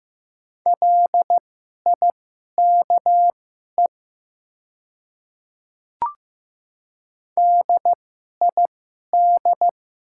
Morse code, heard as a keyed tone, about 15 wpm, spaced as for 11 wpm, 700 Hz. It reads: LIKE DID
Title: 2x Character Spacing - Top 100 Words - 15wpm